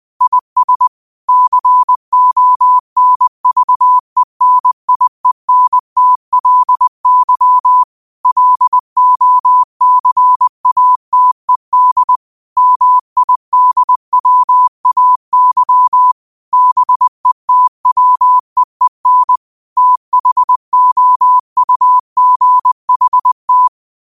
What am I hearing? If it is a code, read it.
Morse code, heard as a keyed tone, about 20 wpm, 1000 Hz. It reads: IS CONVENIENTLY LOCATED MIDWAY BETWEEN THOUGHT